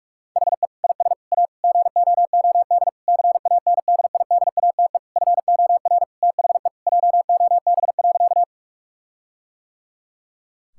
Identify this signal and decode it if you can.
Morse code, heard as a keyed tone, about 41 wpm, 700 Hz. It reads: HE IS A GOOD CANDIDATE FOR THE JOB.
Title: Sentences from Top 1600 Words 45wpm